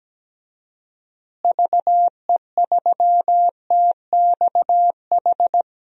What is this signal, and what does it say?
Morse code, heard as a keyed tone, about 17 wpm, 700 Hz. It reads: VE3TXH
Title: DXCC-100 Call Signs - 6 Characters - Repeated 3x - 17wpm